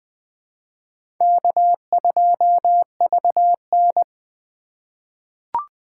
Morse code, heard as a keyed tone, about 20 wpm, 700 Hz. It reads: K2VN
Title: USA Call Signs - 2x Character Spacing - 20wpm